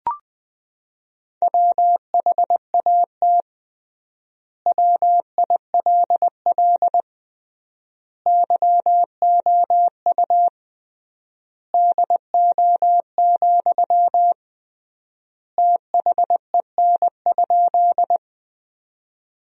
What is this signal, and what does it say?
Morse code, heard as a keyed tone, about 20 wpm, 700 Hz. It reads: WHAT WILL YOU DO, THEN?
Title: 3x Word-Spacing - Sentences from Top 100 Words - 20wpm